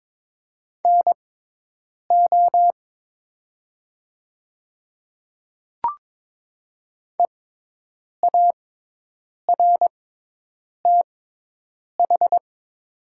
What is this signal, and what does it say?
Morse code, heard as a keyed tone, about 22 wpm, spaced as for 8 wpm, 700 Hz. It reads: NO EARTH
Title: 6x Character Spacing - Top 200 Words - 22wpm